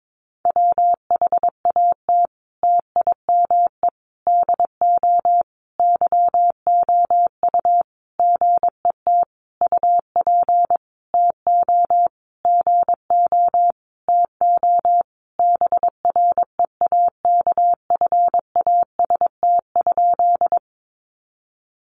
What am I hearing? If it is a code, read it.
Morse code, heard as a keyed tone, about 22 wpm, 700 Hz. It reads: WHAT TIME DO YOU GET UP TO GO TO BREAKFAST?
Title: Sentences from Top 1500 Words 22wpm